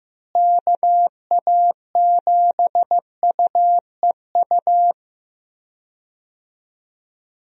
Morse code, heard as a keyed tone, about 16 wpm, 700 Hz. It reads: KA7UEU
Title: USA Call Signs - 6 Characters - Repeated 2x - 15wpm